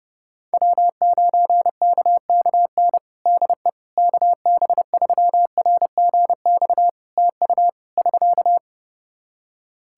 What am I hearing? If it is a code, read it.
Morse code, heard as a keyed tone, about 32 wpm, 700 Hz. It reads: W9KKN DE KB3RGX TU <SK>
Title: Practice QSO - Exchange 4 - Ending - 30wpm